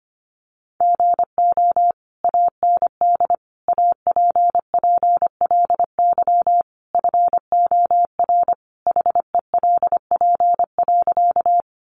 Morse code, heard as a keyed tone, about 25 wpm, 700 Hz. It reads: GO AND APPLY FOR HELP.
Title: Sentences from Top 1200 Words 25wpm